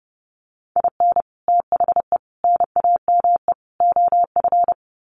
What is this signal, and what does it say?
Morse code, heard as a keyed tone, about 30 wpm, 700 Hz. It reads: IN THE NAME OF
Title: Sets of 4 Words 30wpm